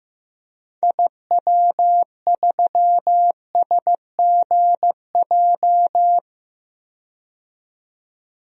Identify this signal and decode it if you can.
Morse code, heard as a keyed tone, about 15 wpm, 700 Hz. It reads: IW3SGJ
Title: DXCC-100 Call Signs - 6 Characters - 15wpm